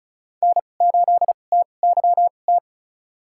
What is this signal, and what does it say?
Morse code, heard as a keyed tone, about 36 wpm, spaced as for 25 wpm, 700 Hz. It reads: N8TYT
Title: USA Call Signs - 5 Characters - 2x Character Spacing - 35wpm